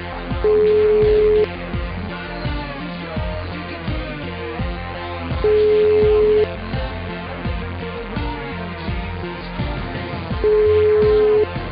Ringback tone (Europe, 425 Hz), with noise.